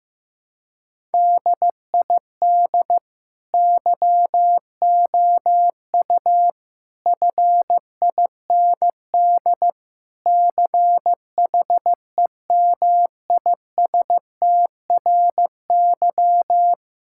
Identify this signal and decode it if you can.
Morse code, heard as a keyed tone, about 15 wpm, 700 Hz. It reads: DID YOU FIND CHEMISTRY
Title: Sentences from Top 1000 Words - Review Part 6 - 15wpm